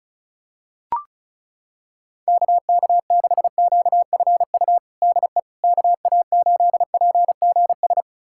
Morse code, heard as a keyed tone, about 36 wpm, 700 Hz. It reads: KK6QFU DE KA8PGS
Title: USA Callsign DE Callsign - 6 Characters - 35wpm